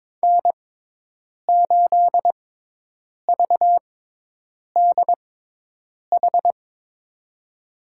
Morse code, heard as a keyed tone, about 22 wpm, spaced as for 8 wpm, 700 Hz. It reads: N8VDH
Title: USA Call Signs - 6x Character Spacing - 22wpm